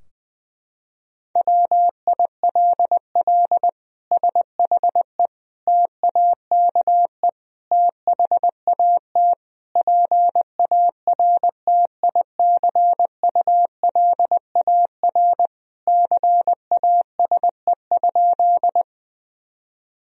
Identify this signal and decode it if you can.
Morse code, heard as a keyed tone, about 20 wpm, 700 Hz. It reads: WILL SHE TAKE THAT PARTICULAR CASE?
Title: Sentences from Top 1000 Words - Review Part 4 - 20wpm